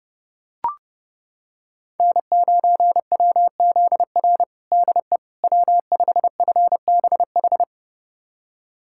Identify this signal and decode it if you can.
Morse code, heard as a keyed tone, about 32 wpm, 700 Hz. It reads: N9WZR DE W5FBH
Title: USA Callsign DE Callsign - 5 Characters - 30wpm